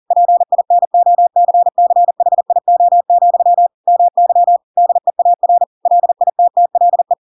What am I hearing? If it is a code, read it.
Morse code, heard as a keyed tone, about 40 wpm, 700 Hz. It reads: PINOCCHIO, MY DEAR LITTLE
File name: Pinocchio / Pinocchio ch7 40 40 ik1hge